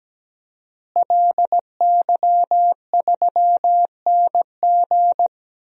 Morse code, heard as a keyed tone, about 17 wpm, 700 Hz. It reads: LY3NG
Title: DXCC-100 Call Signs - 5 Characters - Repeated 2x - 17wpm